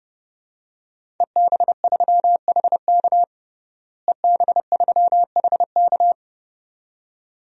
Morse code, heard as a keyed tone, about 32 wpm, 700 Hz. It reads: EB3HK EB3HK
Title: DXCC-100 Call Signs - 5 Characters - Repeated 2x - 30wpm